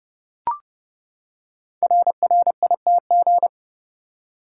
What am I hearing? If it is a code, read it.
Morse code, heard as a keyed tone, about 28 wpm, 700 Hz. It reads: RRITG
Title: Code Groups - 5 Letter:Numbers - 30wpm